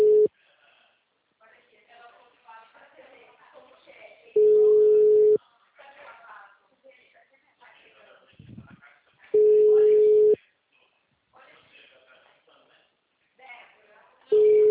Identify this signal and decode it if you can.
Ringback tone (Europe, 425 Hz), with speech.
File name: out- -2008- - - .8